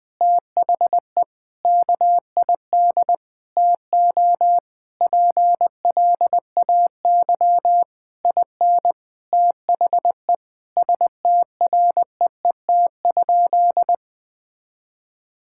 Morse code, heard as a keyed tone, about 20 wpm, 700 Hz. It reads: THE KID TO PLAY IN THE STREET?
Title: Sentences from Top 1600 Words 20wpm